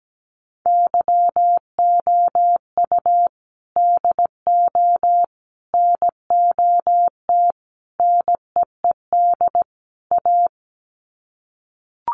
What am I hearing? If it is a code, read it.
Morse code, heard as a keyed tone, about 17 wpm, 700 Hz. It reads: YOU DO NOT NEED A
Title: Sets of 5 Words Encore 17wpm